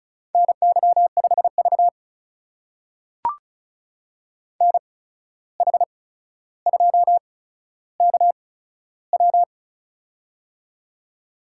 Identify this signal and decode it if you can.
Morse code, heard as a keyed tone, about 36 wpm, spaced as for 14 wpm, 700 Hz. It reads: NY5V NH2KW
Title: USA Call Signs - 8x Character Spacing - 35wpm